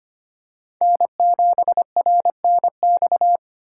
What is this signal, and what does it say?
Morse code, heard as a keyed tone, about 25 wpm, 700 Hz. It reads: N7RNX
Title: USA Call Signs - 5 Characters - 4x Character Spacing - 25wpm